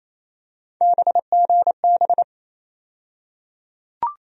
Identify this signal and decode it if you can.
Morse code, heard as a keyed tone, about 28 wpm, 700 Hz. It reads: BGB
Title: USA Call Sign Postfixes - 3 Letters 28wpm